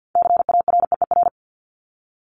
Morse code, heard as a keyed tone, about 50 wpm, 700 Hz. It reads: CAREER
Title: Top 751-1000 Words 50wpm